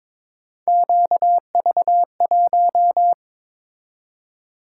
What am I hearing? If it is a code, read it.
Morse code, heard as a keyed tone, about 22 wpm, 700 Hz. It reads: QV1
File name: practice / 22wpm / Three Letter-Number Combinations - 22wpm